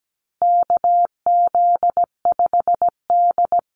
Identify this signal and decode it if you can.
Morse code, heard as a keyed tone, about 17 wpm, 700 Hz. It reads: KZ5D
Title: Call Signs Repeated 2x - 17wpm